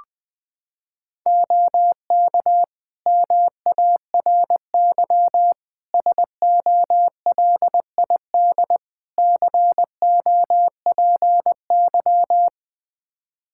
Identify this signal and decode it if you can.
Morse code, heard as a keyed tone, about 20 wpm, 700 Hz. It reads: OK MARY SOLID COPY
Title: Practice QSO - Exchange 3 - More Info - 20wpm